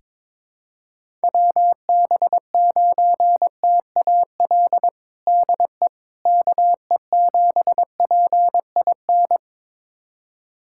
Morse code, heard as a keyed tone, about 22 wpm, 700 Hz. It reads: WB9TAL DE KE7PIN
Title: USA Callsign DE Callsign - 6 Characters - 22wpm